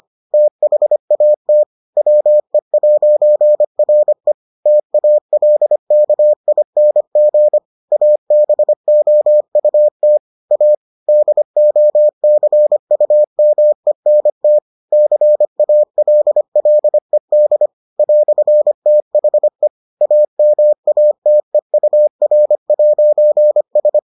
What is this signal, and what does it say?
Morse code, heard as a keyed tone, about 25 wpm, 600 Hz. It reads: THAT WE'RE TALKING ABOUT A DOCUMENT CALLED "THE AMATEUR'S